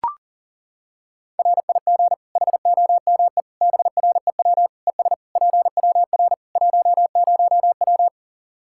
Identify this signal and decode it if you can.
Morse code, heard as a keyed tone, about 40 wpm, 700 Hz. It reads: RIG HOME BREW ES PWR 10W
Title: Practice QSO - Exchange 2 - Info - 40wpm